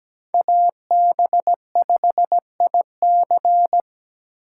Morse code, heard as a keyed tone, about 20 wpm, 700 Hz. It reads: AB5IC
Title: Speed Racing - USA 2x2 Call Signs 17wpm